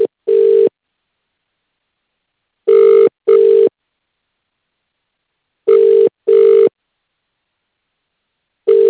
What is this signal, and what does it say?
Ringback tone (UK), 400 + 450 Hz.